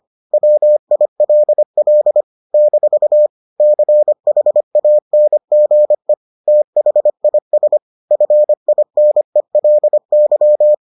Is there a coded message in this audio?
Yes: Morse code from 0.3 to 10.7 s, heard as a keyed tone, about 25 wpm, 600 Hz. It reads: WILL - CHANGE THIS FINELY